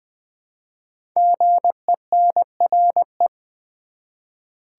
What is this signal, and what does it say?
Morse code, heard as a keyed tone, about 20 wpm, 700 Hz. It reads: GENRE
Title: 5 Letter Words - First 3 Letters Given - 20wpm